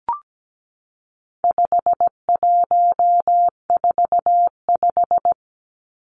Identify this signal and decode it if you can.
Morse code, heard as a keyed tone, about 17 wpm, 700 Hz. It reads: 5145
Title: Numbers - 4-Digit 17wpm